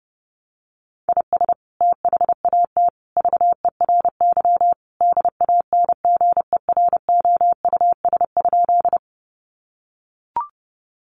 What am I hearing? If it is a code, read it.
Morse code, heard as a keyed tone, about 30 wpm, 700 Hz. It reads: IS THAT VERY DANGEROUS?
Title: Sentences from Top 1200 Words 30wpm